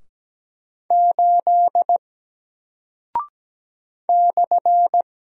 Morse code, heard as a keyed tone, about 17 wpm, 700 Hz. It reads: 8 /
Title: Single Character (Letter:Number:Punctuation:Prosign) 17wpm